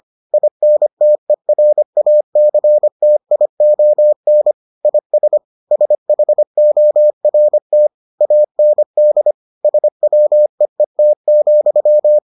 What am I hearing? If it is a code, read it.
Morse code, heard as a keyed tone, about 25 wpm, 600 Hz. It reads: INTERACTION IS SHORT AND SWEET,